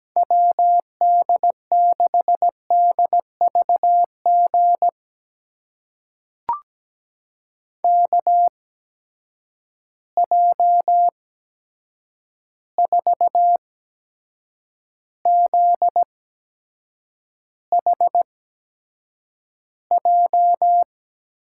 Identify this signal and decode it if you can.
Morse code, heard as a keyed tone, about 17 wpm, spaced as for 7 wpm, 700 Hz. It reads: WD6DVG KJ4ZHJ
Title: USA Call Signs - 6 Characters - 8x Character Spacing - 17wpm